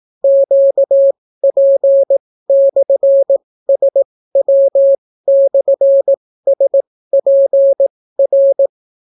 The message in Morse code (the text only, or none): QP/SW/SPR